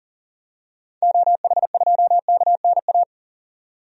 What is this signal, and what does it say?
Morse code, heard as a keyed tone, about 40 wpm, 700 Hz. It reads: OH2KNA
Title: DXCC-100 Call Signs - 6 Characters - 40wpm